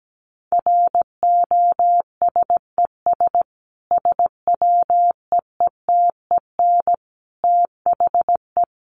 Morse code, heard as a keyed tone, about 17 wpm, 700 Hz. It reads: ROSES SWEETEN THE
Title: A Sound Of Thunder 17wpm